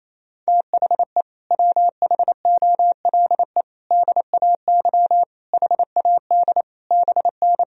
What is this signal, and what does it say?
Morse code, heard as a keyed tone, about 28 wpm, 700 Hz. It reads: THE WHOLE DAY HAD BN
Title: Sentences from Top 500 Words and Abbreviations 28wpm